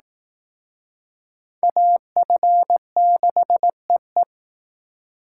Morse code, heard as a keyed tone, about 19 wpm, 700 Hz. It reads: AF6EE
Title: Speed Racing - USA 2x2 Call Signs 15wpm